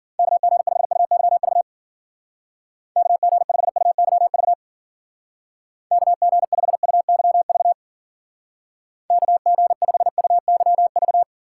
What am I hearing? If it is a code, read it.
Morse code, heard as a keyed tone, about 51 wpm, 700 Hz. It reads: KG5UYV KG5UYV KG5UYV KG5UYV